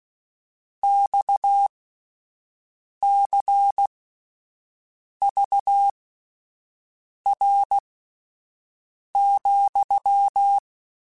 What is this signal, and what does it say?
Morse code, heard as a keyed tone, about 16 wpm, spaced as for 5 wpm, 800 Hz. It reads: XCVR,